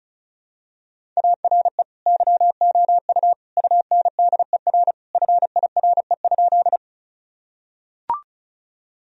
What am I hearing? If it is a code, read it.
Morse code, heard as a keyed tone, about 36 wpm, 700 Hz. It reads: ARE YOU UNDER FIRE?
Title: Sentences from Top 1000 Words - Review Part 1 - 35wpm